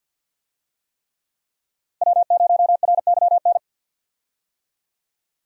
Morse code, heard as a keyed tone, about 51 wpm, 700 Hz. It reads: W0RYN
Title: USA Call Signs - 5 Characters - Repeated 3x - 50wpm